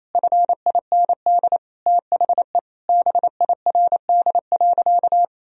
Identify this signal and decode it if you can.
Morse code, heard as a keyed tone, about 28 wpm, 700 Hz. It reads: FIND THE BIRD.
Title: Sentences from Top 1000 Words - Review Part 2 - 28wpm